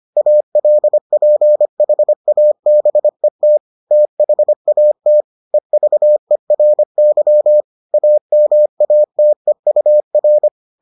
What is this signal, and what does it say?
Morse code, heard as a keyed tone, about 25 wpm, 600 Hz. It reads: ALPHABET THAT EVERY AMATEUR